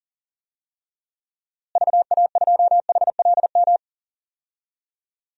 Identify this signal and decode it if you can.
Morse code, heard as a keyed tone, about 40 wpm, 700 Hz. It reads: VA2HLM
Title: DXCC-100 Call Signs - 6 Characters - Repeated 3x - 40wpm